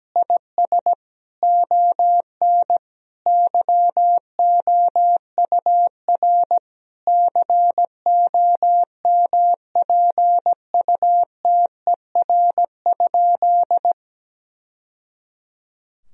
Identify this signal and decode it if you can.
Morse code, heard as a keyed tone, about 17 wpm, 700 Hz. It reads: IS ON YOUR COMPUTER?
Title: Sentences from Top 1000 Words - Review Part 3 - 17wpm